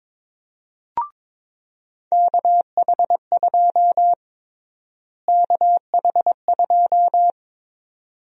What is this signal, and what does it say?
Morse code, heard as a keyed tone, about 22 wpm, 700 Hz. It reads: KH2 KH2
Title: USA Call Sign Prefixes - Repeated 2x - 22wpm